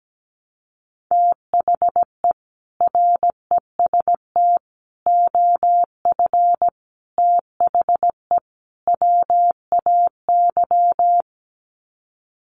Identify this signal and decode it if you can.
Morse code, heard as a keyed tone, about 17 wpm, 700 Hz. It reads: THE REST OF THE WAY
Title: Sets of 5 Words Encore 17wpm